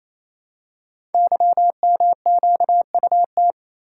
Morse code, heard as a keyed tone, about 27 wpm, 700 Hz. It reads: YMQUT
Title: Code Groups - 5 Letters - 28wpm